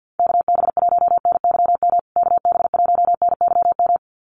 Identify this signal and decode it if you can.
Morse code, heard as a keyed tone, about 50 wpm, 700 Hz. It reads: KB1NYM KB1NYM
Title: USA Call Signs Repeated 2x 50wpm